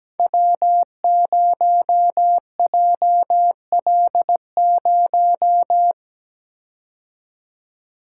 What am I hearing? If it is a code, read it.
Morse code, heard as a keyed tone, about 17 wpm, 700 Hz. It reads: W0JL0
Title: Code Groups - 5 Letter:Numbers - 17wpm